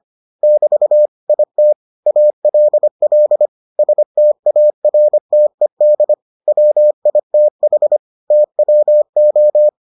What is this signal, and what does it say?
Morse code, heard as a keyed tone, about 25 wpm, 600 Hz. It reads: <BT> IT ALL STARTED WITH TWO